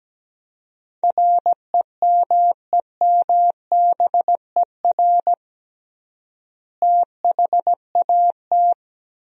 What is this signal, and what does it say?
Morse code, heard as a keyed tone, about 17 wpm, 700 Hz. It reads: REMEMBER THAT
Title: 3x Word-Spacing - Sentences from Top 400 Words - 17wpm